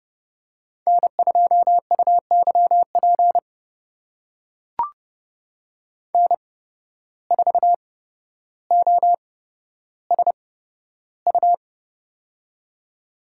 Morse code, heard as a keyed tone, about 30 wpm, spaced as for 13 wpm, 700 Hz. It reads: N2UYP N4OSU
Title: USA Call Signs - 5 Characters - 8x Character Spacing - 30wpm